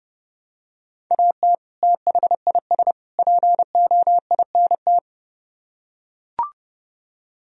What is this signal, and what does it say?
Morse code, heard as a keyed tone, about 32 wpm, 700 Hz. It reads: AT THIS POINT